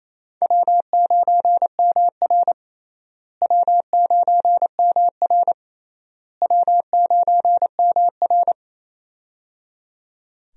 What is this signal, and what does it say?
Morse code, heard as a keyed tone, about 28 wpm, 700 Hz. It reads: W9MR W9MR W9MR
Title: USA Call Signs - 4 Characters - Repeated 3x - 28wpm